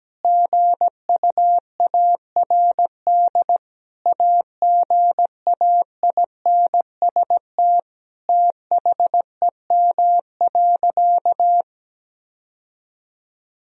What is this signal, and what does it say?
Morse code, heard as a keyed tone, about 17 wpm, 700 Hz. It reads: GUARD AGAINST THEM.
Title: Sentences from Top 1000 Words - Review Part 4 - 17wpm